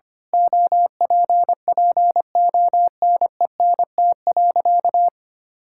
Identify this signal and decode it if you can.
Morse code, heard as a keyed tone, about 25 wpm, 700 Hz. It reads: OPPONENT.
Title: Pangram Challenge v2 - 25wpm